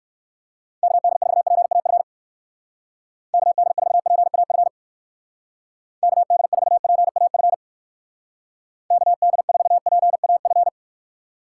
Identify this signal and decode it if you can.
Morse code, heard as a keyed tone, about 58 wpm, 700 Hz. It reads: KD4PAF KD4PAF KD4PAF KD4PAF